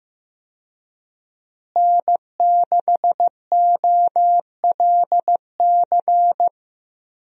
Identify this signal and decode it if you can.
Morse code, heard as a keyed tone, about 15 wpm, 700 Hz. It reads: N6OLC
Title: USA Call Signs - 5 Characters - Repeated 2x - 15wpm